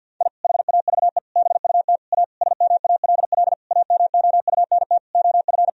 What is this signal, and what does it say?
Morse code, heard as a keyed tone, about 47 wpm, 700 Hz. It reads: I HAVE BUT A SMALL AMOUNT OF
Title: Sentences from Top 1000 Words - Review Part 4 - 50wpm